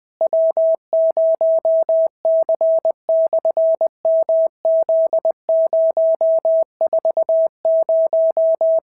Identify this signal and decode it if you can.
Morse code, heard as a keyed tone, about 20 wpm, 650 Hz. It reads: W0C/MZ040
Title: SOTA QSO Practice 20wpm